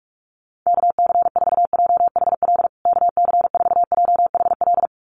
Morse code, heard as a keyed tone, about 44 wpm, 700 Hz. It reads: KC4JHL KC4JHL
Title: USA Call Signs Repeated 2x 45wpm